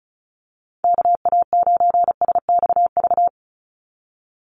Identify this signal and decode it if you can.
Morse code, heard as a keyed tone, about 34 wpm, 700 Hz. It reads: KA9SXV
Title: Call Signs Repeated 3x - 35wpm